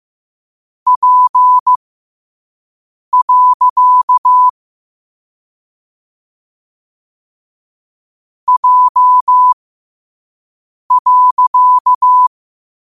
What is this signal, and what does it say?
Morse code, heard as a keyed tone, about 15 wpm, spaced as for 5 wpm, 1000 Hz. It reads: P. J.
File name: 5wpm